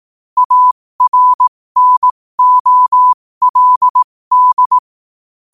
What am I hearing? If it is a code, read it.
Morse code, heard as a keyed tone, about 18 wpm, spaced as for 15 wpm, 1000 Hz. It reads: ARNOLD
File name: qotdfiles / 15wpm